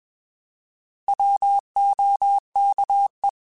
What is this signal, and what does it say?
Morse code, heard as a keyed tone, about 21 wpm, 800 Hz. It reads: WOKE
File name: Alphabetical / Alphabetical mp3s / Ws @20